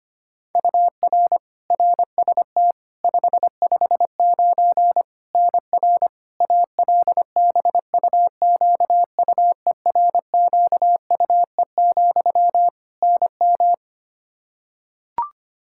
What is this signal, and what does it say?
Morse code, heard as a keyed tone, about 25 wpm, 700 Hz. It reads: UR RST 559 NR ALBUQUERQUE, NM
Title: Practice QSO - Exchange 1 - Beginning - 25wpm